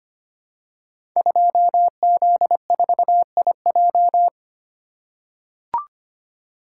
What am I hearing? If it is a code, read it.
Morse code, heard as a keyed tone, about 25 wpm, 700 Hz. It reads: 2Z4IJ
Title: Code Groups - 5 Letter:Numbers - 25wpm